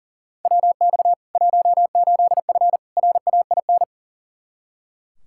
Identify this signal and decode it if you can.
Morse code, heard as a keyed tone, about 40 wpm, 700 Hz. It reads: WX 18F RAIN
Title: Practice QSO - Exchange 2 - Info - 40wpm